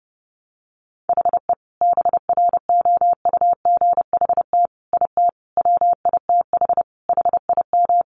Morse code, heard as a keyed tone, about 30 wpm, 700 Hz. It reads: HE BROUGHT IT WITH HIM
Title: Sentences from Top 500 Words 30wpm